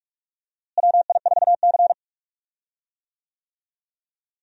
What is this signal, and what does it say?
Morse code, heard as a keyed tone, about 42 wpm, 700 Hz. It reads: WI4C
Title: USA Call Signs - 4 Characters - Repeated 3x - 45wpm